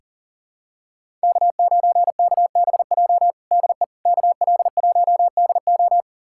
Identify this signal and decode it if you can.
Morse code, heard as a keyed tone, about 40 wpm, 700 Hz. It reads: K9KBJ DE KL1DO
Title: USA Callsign DE Callsign - 5 Characters - 40wpm